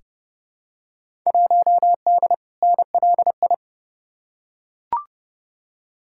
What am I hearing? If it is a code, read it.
Morse code, heard as a keyed tone, about 30 wpm, 700 Hz. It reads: 1D NLI
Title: Field Day Exchanges - 30wpm